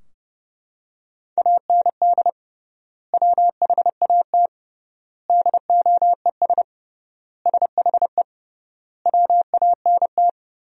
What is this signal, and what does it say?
Morse code, heard as a keyed tone, about 30 wpm, 700 Hz. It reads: AND WHAT DOES SHE WANT
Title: 3x Word-Spacing - Sentences from Top 400 Words - 30wpm